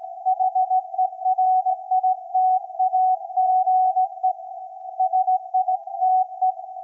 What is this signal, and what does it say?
Morse code, heard as a keyed tone, about 17 wpm, 750 Hz. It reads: HERITAGE SITE
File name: iz7slz / VLF / SAQ 24-12-2024